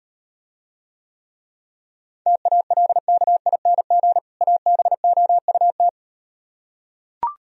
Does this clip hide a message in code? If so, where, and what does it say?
Yes: Morse code from 2.3 to 5.9 s, heard as a keyed tone, about 38 wpm, 700 Hz. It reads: TALKING ABOUT